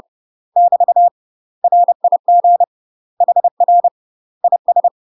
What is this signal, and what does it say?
Morse code, heard as a keyed tone, about 30 wpm, 700 Hz. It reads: <BT> RIG HR IS
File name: UK MASTER QSOs TRAINING / 30WPM / UK MASTER QSOs 0016